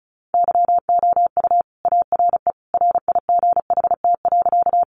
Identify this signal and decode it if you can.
Morse code, heard as a keyed tone, about 34 wpm, 700 Hz. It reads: YOU ARE RIGHT.
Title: Sentences from Top 200 Words 35wpm